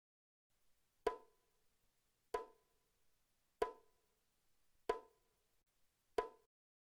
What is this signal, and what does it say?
There is no code.